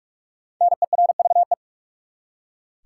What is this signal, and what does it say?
Morse code, heard as a keyed tone, about 44 wpm, 700 Hz. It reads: NERVE
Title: Five Letter Words - 45wpm